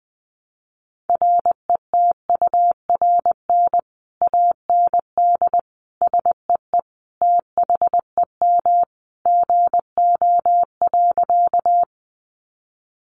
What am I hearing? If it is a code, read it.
Morse code, heard as a keyed tone, about 20 wpm, 700 Hz. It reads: RETURN AND SEE THEM GO.
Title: Sentences from Top 800 Words 20wpm